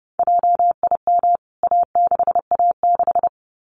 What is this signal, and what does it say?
Morse code, heard as a keyed tone, about 28 wpm, 700 Hz. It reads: JIM A6A6
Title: CWT Exchange 30wpm